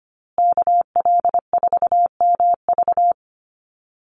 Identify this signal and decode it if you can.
Morse code, heard as a keyed tone, about 25 wpm, 700 Hz. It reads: KL4MV
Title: USA Call Signs Repeated 2x 25wpm